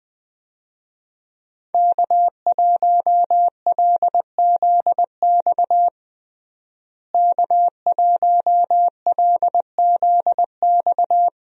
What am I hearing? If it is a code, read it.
Morse code, heard as a keyed tone, about 20 wpm, 700 Hz. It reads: K1LZX K1LZX
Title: USA Call Signs - 5 Characters - Repeated 2x - 20wpm